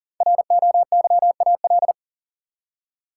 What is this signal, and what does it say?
Morse code, heard as a keyed tone, about 40 wpm, 700 Hz. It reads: ROYAL
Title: Five Letter Words - 40wpm